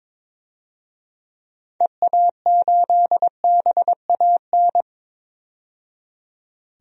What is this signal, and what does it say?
Morse code, heard as a keyed tone, about 22 wpm, 700 Hz. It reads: EA8BAN
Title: DXCC-100 Call Signs - 6 Characters - 22wpm